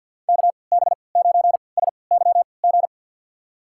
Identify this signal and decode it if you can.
Morse code, heard as a keyed tone, about 53 wpm, spaced as for 29 wpm, 700 Hz. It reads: KB9SYG
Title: USA Call Signs - 6 Characters - 3x Character Spacing - 50wpm